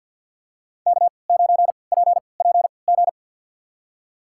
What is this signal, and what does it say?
Morse code, heard as a keyed tone, about 52 wpm, spaced as for 28 wpm, 700 Hz. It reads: K9PPG
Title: USA Call Signs - 5 Characters - 3x Character Spacing - 50wpm